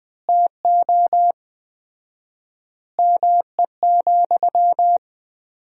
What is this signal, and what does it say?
Morse code, heard as a keyed tone, about 20 wpm, 700 Hz. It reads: TO ME,
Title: 4x Word-Spacing - Sentences from Top 500 Words - 20wpm